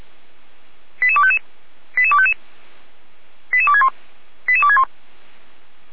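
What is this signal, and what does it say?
Five-tone selective call (ZVEI), heard as quick runs of five tones, tones of about 70 ms. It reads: 88255 88255 88261 88261
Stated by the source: ZVEI five-tone call 88231 received on an SDR.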